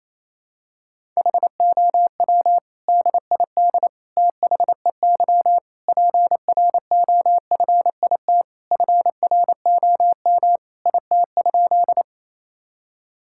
Morse code, heard as a keyed tone, about 28 wpm, 700 Hz. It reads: HOW DID THEY PROFIT FROM IT?
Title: Sentences from Top 1000 Words - Review Part 5 - 28wpm